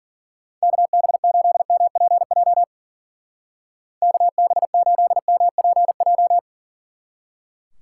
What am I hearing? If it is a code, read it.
Morse code, heard as a keyed tone, about 49 wpm, 700 Hz. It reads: KB8MPJ KB8MPJ
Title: Speed Racing - USA 2x3 Call Signs 40wpm